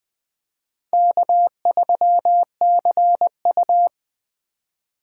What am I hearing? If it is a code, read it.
Morse code, heard as a keyed tone, about 20 wpm, 700 Hz. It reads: K3CU
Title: 2023 POTA Hunter Call Signs - 20wpm